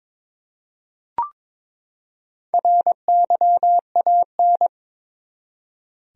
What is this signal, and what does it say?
Morse code, heard as a keyed tone, about 21 wpm, 700 Hz. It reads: RYAN
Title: USA Common Names - 22wpm